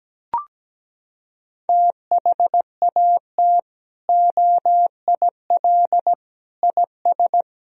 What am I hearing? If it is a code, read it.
Morse code, heard as a keyed tone, about 17 wpm, 700 Hz. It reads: THAT OIL IS
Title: Sentences from Top 1000 Words - Review Part 4 - 17wpm